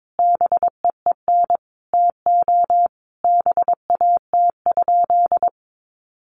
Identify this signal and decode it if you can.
Morse code, heard as a keyed tone, about 22 wpm, 700 Hz. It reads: BEEN TO BAT?
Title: Sentences from Top 1500 Words 22wpm